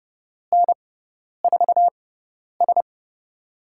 Morse code, heard as a keyed tone, about 32 wpm, spaced as for 10 wpm, 700 Hz. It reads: N4S
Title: USA Call Signs - 6x Character Spacing - 30wpm